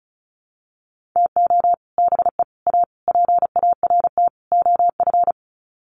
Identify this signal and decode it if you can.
Morse code, heard as a keyed tone, about 34 wpm, 700 Hz. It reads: TO BE A PART OF
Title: Sets of 5 Words 35wpm